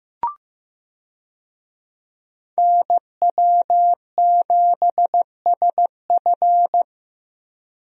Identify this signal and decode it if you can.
Morse code, heard as a keyed tone, about 15 wpm, 700 Hz. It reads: NW7SF
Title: USA Call Signs - 5 Characters - Repeated 3x - 15wpm